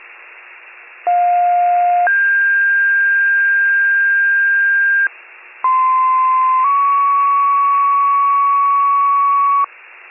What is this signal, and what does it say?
Two-tone page, heard as a one-second tone, then a three-second tone, about 710 then 1600 Hz, about 1040 then 1120 Hz. It is heard from 1.1 to 9.6 s.